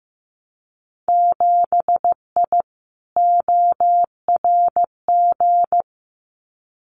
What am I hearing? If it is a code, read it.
Morse code, heard as a keyed tone, about 15 wpm, 700 Hz. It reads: 7I ORG
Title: Winter Field Day Exchanges Encore 15wpm